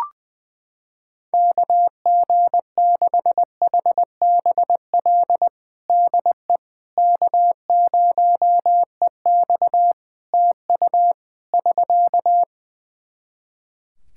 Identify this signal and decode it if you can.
Morse code, heard as a keyed tone, about 20 wpm, 700 Hz. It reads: KG6HBL DE K0EX TU <SK>
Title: K1USN SST QSO Practice 20wpm